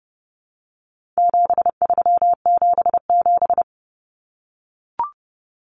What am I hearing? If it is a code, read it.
Morse code, heard as a keyed tone, about 30 wpm, 700 Hz. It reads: 7377
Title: Common Cut Numbers - 4-Digit 30wpm